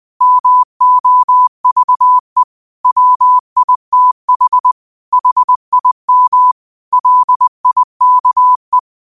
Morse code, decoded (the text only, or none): MOVE WITH HIM LIKE